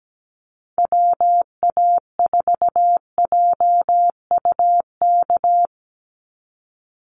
Morse code, heard as a keyed tone, about 17 wpm, 700 Hz. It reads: WA4JUK
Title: Call Signs Repeated 3x - 17wpm